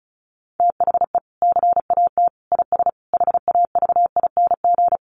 Morse code, heard as a keyed tone, about 34 wpm, 700 Hz. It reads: THE CAT IS HAVING
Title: Easy Phrases 35wpm